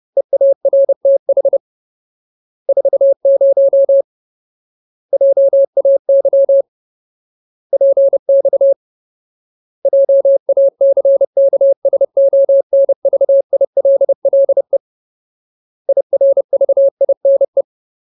Morse code, heard as a keyed tone, about 30 wpm, 550 Hz. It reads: EARTH 40 JAY PX JACKSONVILLE IRVINE
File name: Ultimate Word List / 30WPM Ultimate